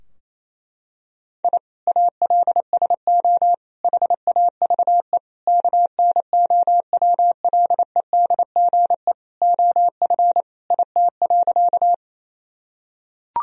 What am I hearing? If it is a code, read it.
Morse code, heard as a keyed tone, about 28 wpm, 700 Hz. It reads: I ALSO HAVE KNOWLEDGE OF IT.
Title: Sentences from Top 1000 Words - Review Part 3 - 28wpm